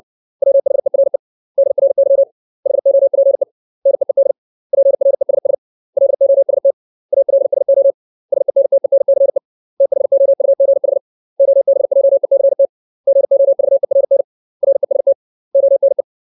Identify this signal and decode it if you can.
Morse code, heard as a keyed tone, about 60 wpm, 550 Hz. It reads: WHERE DRY HOPE NEED PRESS LOST ALSO SENTENCE THOUGH OBJECT GOVERN REST ONE
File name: 632 words / 632 Words 60WPM